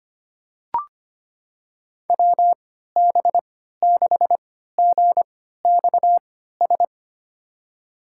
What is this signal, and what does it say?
Morse code, heard as a keyed tone, about 25 wpm, spaced as for 14 wpm, 700 Hz. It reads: WB6GXS
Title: USA Call Signs - 6 Characters - 3x Character Spacing - 25wpm